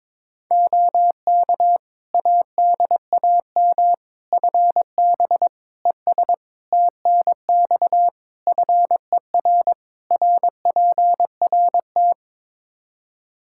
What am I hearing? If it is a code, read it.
Morse code, heard as a keyed tone, about 21 wpm, 700 Hz. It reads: OK ADAM FB ES TNX FER RPRT
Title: Practice QSO - Exchange 2 - Info - 22wpm